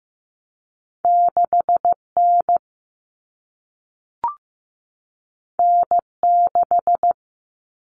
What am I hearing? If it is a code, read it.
Morse code, heard as a keyed tone, about 15 wpm, 700 Hz. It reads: 6 N N 6
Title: Common Cut Numbers - 2-Digit 15wpm